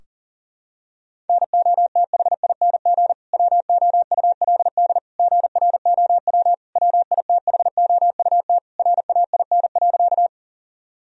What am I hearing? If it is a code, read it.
Morse code, heard as a keyed tone, about 40 wpm, 700 Hz. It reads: NOTHING WOULD GROW WITHOUT RAIN.
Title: Sentences from Top 1000 Words - Review Part 3 - 40wpm